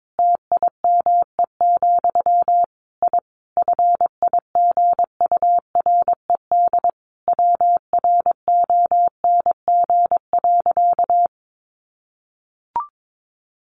Morse code, heard as a keyed tone, about 22 wpm, 700 Hz. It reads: TIME, I FIGURED WRONG.
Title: A Sound Of Thunder 22wpm